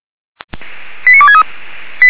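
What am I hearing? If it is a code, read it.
Five-tone selective call (ZVEI), heard as quick runs of five tones, tones of about 70 ms. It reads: 89263
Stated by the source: ZVEI five-tone selective call received on SDR.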